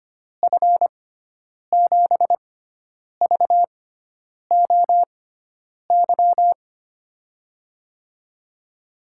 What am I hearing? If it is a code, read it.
Morse code, heard as a keyed tone, about 25 wpm, spaced as for 9 wpm, 700 Hz. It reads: F7VOY
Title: USA Call Signs - 6 Characters - 6x Character Spacing - 25wpm